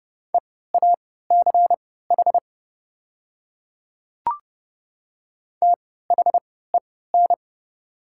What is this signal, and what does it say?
Morse code, heard as a keyed tone, about 30 wpm, spaced as for 17 wpm, 700 Hz. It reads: EACH THEN
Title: 3x Character Spacing - Top 100 Words - 30wpm